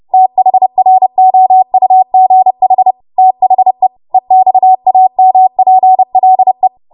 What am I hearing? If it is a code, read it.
Morse code, heard as a keyed tone, about 30 wpm, 750 Hz. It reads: THROUGH THE EXAMPLE
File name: Morse / 30WPM